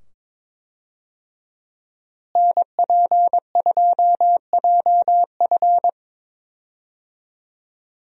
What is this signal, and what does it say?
Morse code, heard as a keyed tone, about 21 wpm, 700 Hz. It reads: NP2JF
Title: USA Call Signs - 5 Characters - Repeated 2x - 22wpm